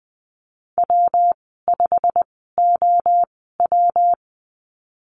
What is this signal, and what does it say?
Morse code, heard as a keyed tone, about 20 wpm, spaced as for 15 wpm, 700 Hz. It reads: W5OW
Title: 122 Lesson 20g - Number 3 - Call Signs 20wpm